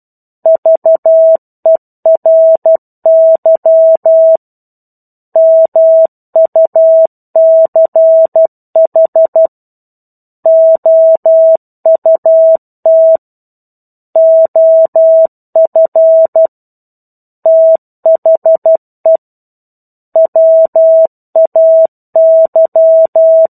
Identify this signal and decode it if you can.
Morse code, heard as a keyed tone, about 12 wpm, 650 Hz. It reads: VERY MUCH OUT OF THE WAY
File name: 01-Chapter-01 12 WPM